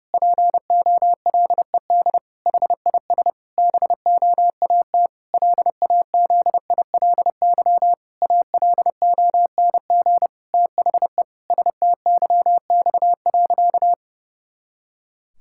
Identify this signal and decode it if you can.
Morse code, heard as a keyed tone, about 30 wpm, 700 Hz. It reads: POLED HIS BOAT LAZILY ALONG THE STYX.
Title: Pangram Challenge v2 - 30wpm